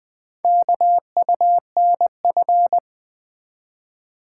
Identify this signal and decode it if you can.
Morse code, heard as a keyed tone, about 20 wpm, 700 Hz. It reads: KUNF
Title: Four Letter-Number Combinations - 20wpm